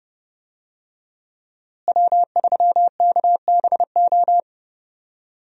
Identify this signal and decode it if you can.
Morse code, heard as a keyed tone, about 30 wpm, 700 Hz. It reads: W3KBO
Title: USA Call Signs - 5 Characters - Repeated 3x - 30wpm